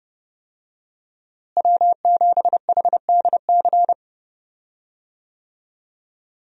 Morse code, heard as a keyed tone, about 30 wpm, 700 Hz. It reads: W7HDC
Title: USA Call Signs - 5 Characters - Repeated 2x - 30wpm